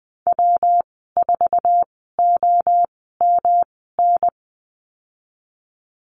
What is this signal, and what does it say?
Morse code, heard as a keyed tone, about 20 wpm, spaced as for 15 wpm, 700 Hz. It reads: W4OMN
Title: 147 Lesson 24g - Letter y - Call Signs 20wpm